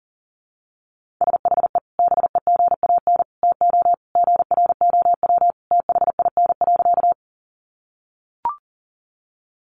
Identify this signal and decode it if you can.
Morse code, heard as a keyed tone, about 40 wpm, 700 Hz. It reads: SHE BEGAN TO GROW THIN.
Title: Sentences from Top 1200 Words 40wpm